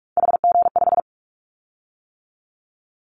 Morse code, heard as a keyed tone, about 44 wpm, 700 Hz. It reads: HG5
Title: Callsign Prefixes for Top 50 DXCC - 45wpm